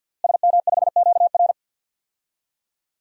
Morse code, heard as a keyed tone, about 53 wpm, 700 Hz. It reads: SM5QR
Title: DXCC-100 Call Signs - 5 Characters - Repeated 3x - 50wpm